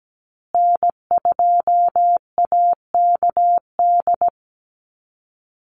Morse code, heard as a keyed tone, about 17 wpm, 700 Hz. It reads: N2AKD
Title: USA Call Signs Repeated 3x 17wpm